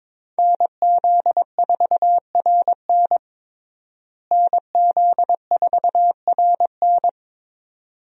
Morse code, heard as a keyed tone, about 22 wpm, 700 Hz. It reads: NZ4RN NZ4RN
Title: USA Call Signs - 5 Characters - Repeated 2x - 22wpm